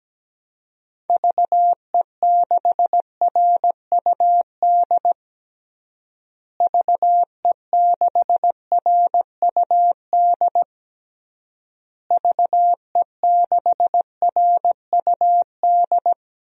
Morse code, heard as a keyed tone, about 17 wpm, 700 Hz. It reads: VE6RUD VE6RUD VE6RUD
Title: DXCC-100 Call Signs - 6 Characters - Repeated 3x - 17wpm